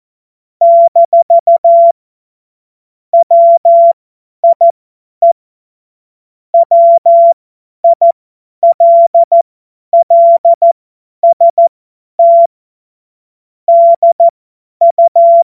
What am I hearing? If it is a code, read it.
Morse code, heard as a keyed tone, about 14 wpm, spaced as for 10 wpm, 680 Hz. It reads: - WIE WILLST DU
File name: Witz-29-Juli-2025-50-F70